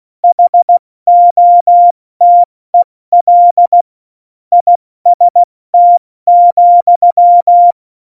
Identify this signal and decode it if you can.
Morse code, heard as a keyed tone, about 16 wpm, 700 Hz. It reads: HOTEL IST,
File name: Morsekurs / Witze / Witz 17. November 2020, 70 F80